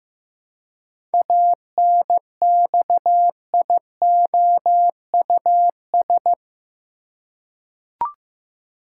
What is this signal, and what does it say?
Morse code, heard as a keyed tone, about 16 wpm, 700 Hz. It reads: ANXIOUS
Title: Seven Letter Words - 15wpm